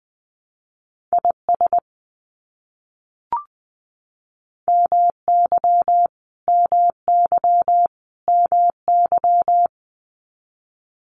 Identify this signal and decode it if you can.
Morse code, heard as a keyed tone, about 20 wpm, 700 Hz. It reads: IS MY MY MY
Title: Top 500 Words Repeated 20wpm